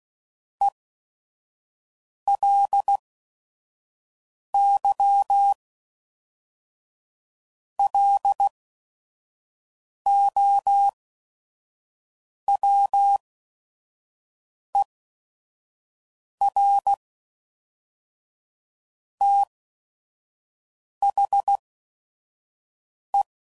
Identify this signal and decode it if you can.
Morse code, heard as a keyed tone, about 16 wpm, spaced as for 5 wpm, 800 Hz. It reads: ELY LOWER THE
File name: CW Code Groups @5WPM / Technical Subjects / XC